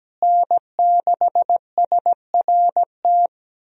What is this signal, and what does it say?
Morse code, heard as a keyed tone, about 17 wpm, 700 Hz. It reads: N6SRT
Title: USA Call Signs - 5 Characters - Repeated 2x - 17wpm